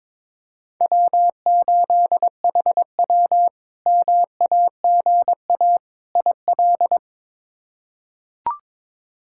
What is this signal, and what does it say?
Morse code, heard as a keyed tone, about 22 wpm, 700 Hz. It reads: W8HW MAGA IL
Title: K1USN SST QSO Practice - 22wpm